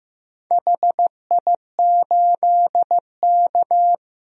Morse code, heard as a keyed tone, about 15 wpm, 700 Hz. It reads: HI8K
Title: DXCC-100 Call Signs - 4 Characters - 15wpm